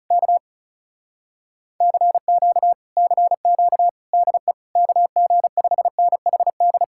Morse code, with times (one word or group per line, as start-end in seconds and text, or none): 0.1-0.4 K
1.8-2.7 CQ
3.0-3.9 CQ
4.1-4.5 DE
4.8-6.8 KG5NHD